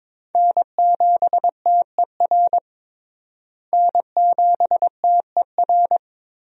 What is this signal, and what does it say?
Morse code, heard as a keyed tone, about 22 wpm, 700 Hz. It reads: N7TER N7TER
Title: USA Call Signs - 5 Characters - Repeated 3x - 22wpm